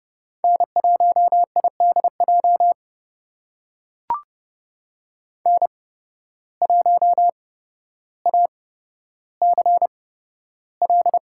Morse code, heard as a keyed tone, about 32 wpm, spaced as for 13 wpm, 700 Hz. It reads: N1IDJ N1ACL
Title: USA Call Signs - 8x Character Spacing - 30wpm